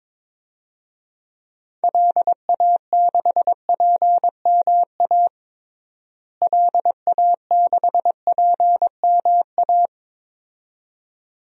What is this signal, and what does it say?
Morse code, heard as a keyed tone, about 22 wpm, 700 Hz. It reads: LA6PMA LA6PMA
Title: DXCC-100 Call Signs - 6 Characters - Repeated 2x - 22wpm